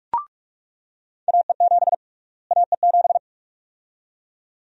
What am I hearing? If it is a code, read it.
Morse code, heard as a keyed tone, about 42 wpm, 700 Hz. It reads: AE7 AE7
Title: USA Call Sign Prefixes - Repeated 2x - 45wpm